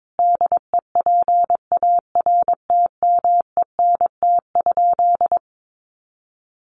Morse code, heard as a keyed tone, about 22 wpm, 700 Hz. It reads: DEPARTMENT?
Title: Sentences from Top 1200 Words 22wpm